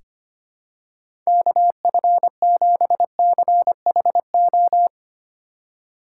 Morse code, heard as a keyed tone, about 25 wpm, 700 Hz. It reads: KF7CHO
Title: USA Call Signs - 6 Characters - 4x Character Spacing - 25wpm